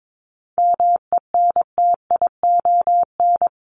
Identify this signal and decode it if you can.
Morse code, heard as a keyed tone, about 22 wpm, 700 Hz. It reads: MENTION
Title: Top 1501-1750 Words 22wpm